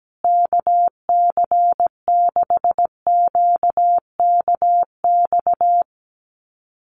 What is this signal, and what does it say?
Morse code, heard as a keyed tone, about 17 wpm, 700 Hz. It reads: KC6QKX
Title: USA Call Signs Repeated 3x 17wpm